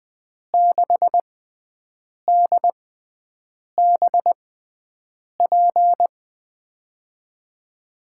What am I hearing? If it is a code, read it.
Morse code, heard as a keyed tone, about 20 wpm, spaced as for 7 wpm, 700 Hz. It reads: 6DBP
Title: USA Call Signs - 6 Characters - 6x Character Spacing - 20wpm